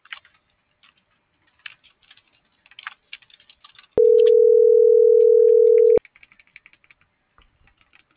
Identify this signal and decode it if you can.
Ringback tone (North America), 440 + 480 Hz.
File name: CENTRAL SOTOVA / out alejandro.rodriguez - .51599